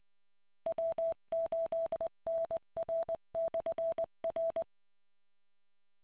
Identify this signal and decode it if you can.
Morse code, heard as a keyed tone, about 20 wpm, 660 Hz. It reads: W8NR/R